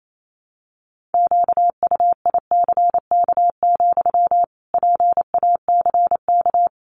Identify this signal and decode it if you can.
Morse code, heard as a keyed tone, about 28 wpm, 700 Hz. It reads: QUICK, PACK
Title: Sentences from Top 1300 Words 28wpm